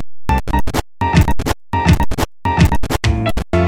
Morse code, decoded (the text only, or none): none